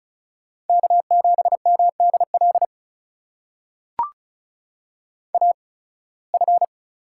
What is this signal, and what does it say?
Morse code, heard as a keyed tone, about 36 wpm, spaced as for 23 wpm, 700 Hz. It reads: K7MDL AF
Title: USA Call Signs - 5 Characters - 8x Character Spacing - 35wpm